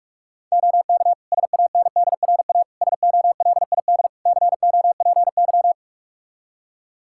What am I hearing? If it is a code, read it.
Morse code, heard as a keyed tone, about 45 wpm, 700 Hz. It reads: OK SANDRA SOLID COPY
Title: Practice QSO - Exchange 3 - More Info - 45wpm